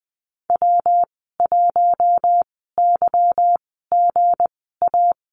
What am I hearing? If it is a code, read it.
Morse code, heard as a keyed tone, about 20 wpm, spaced as for 15 wpm, 700 Hz. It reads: W1YGA
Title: 167 Lesson 27g - Number 7 - Call Signs 20wpm